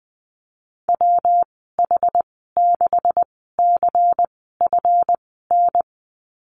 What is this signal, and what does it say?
Morse code, heard as a keyed tone, about 20 wpm, spaced as for 14 wpm, 700 Hz. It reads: WH6CFN
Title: 147 Lesson 24g - Letter y - Call Signs 20wpm